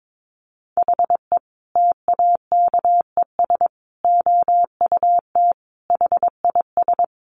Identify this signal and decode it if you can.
Morse code, heard as a keyed tone, about 22 wpm, 700 Hz. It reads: HE TAKES OUT HIS
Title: Additional Easy Phrases 22wpm